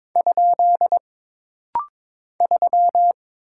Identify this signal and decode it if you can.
Morse code, heard as a keyed tone, about 22 wpm, 700 Hz. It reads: ? 3
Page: This rapid-fire timing is geared to push your ICR (Instant Character Recognition) skills.